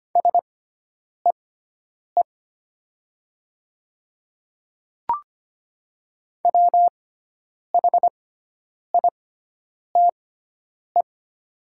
Morse code, heard as a keyed tone, about 25 wpm, spaced as for 9 wpm, 700 Hz. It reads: SEE WHITE
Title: 6x Character Spacing - Top 300 Words - 25wpm